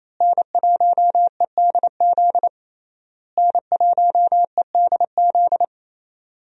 Morse code, heard as a keyed tone, about 28 wpm, 700 Hz. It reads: N1EDZ N1EDZ
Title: USA Call Signs - 5 Characters - Repeated 3x - 28wpm